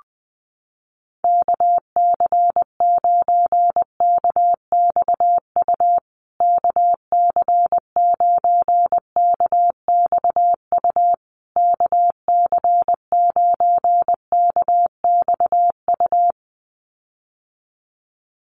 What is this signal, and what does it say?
Morse code, heard as a keyed tone, about 20 wpm, 700 Hz. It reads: KC9KXU KC9KXU KC9KXU
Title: USA Call Signs Repeated 3x 20wpm